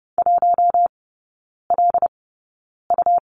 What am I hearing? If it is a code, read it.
Morse code, heard as a keyed tone, about 30 wpm, spaced as for 9 wpm, 700 Hz. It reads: 1LU
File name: Lesson 18e - Letter m - Three Characters 30wpm